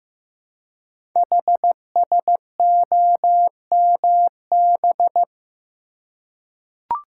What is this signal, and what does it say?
Morse code, heard as a keyed tone, about 15 wpm, 700 Hz. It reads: HSOMB
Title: Code Groups - 5 Letters - 15wpm